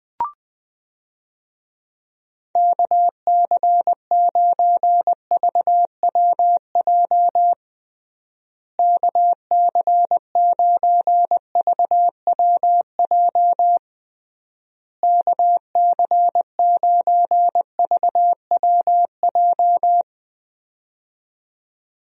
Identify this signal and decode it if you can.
Morse code, heard as a keyed tone, about 20 wpm, 700 Hz. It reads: KC9VWJ KC9VWJ KC9VWJ
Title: USA Call Signs - 6 Characters - Repeated 3x - 20wpm